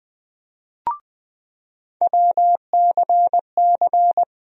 Morse code, heard as a keyed tone, about 21 wpm, 700 Hz. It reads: WCC
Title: USA Call Sign Postfixes - 3 Letters 20wpm